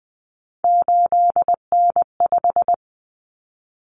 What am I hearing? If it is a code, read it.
Morse code, heard as a keyed tone, about 20 wpm, 700 Hz. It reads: 8N5
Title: Common Cut Numbers - 3-Digit 20wpm